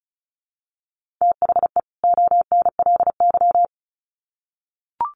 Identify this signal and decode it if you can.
Morse code, heard as a keyed tone, about 34 wpm, 700 Hz. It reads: THE ONLY
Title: Sets of 2 Words 35wpm